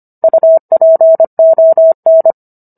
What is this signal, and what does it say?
Morse code, heard as a keyed tone, about 25 wpm, 650 Hz. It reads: UPON